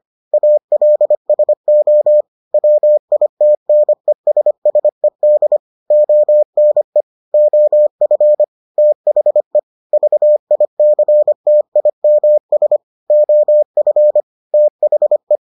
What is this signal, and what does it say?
Morse code, heard as a keyed tone, about 25 wpm, 600 Hz. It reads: ALSO WITNESSED ONE OF THE VICTIMS OF THE